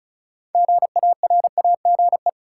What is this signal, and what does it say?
Morse code, heard as a keyed tone, about 36 wpm, 700 Hz. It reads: GARAGE
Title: Sentences from Top 1600 Words - New Word Drill Alt - 35wpm